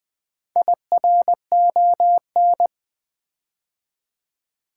Morse code, heard as a keyed tone, about 20 wpm, 700 Hz. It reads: IRON
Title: Build A Word - 4 Letters At A Time - 10-Letter Words - 20wpm